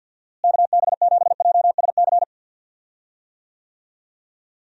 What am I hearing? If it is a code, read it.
Morse code, heard as a keyed tone, about 50 wpm, 700 Hz. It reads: KB7JSC
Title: USA Call Signs - 6 Characters - Repeated 2x - 50wpm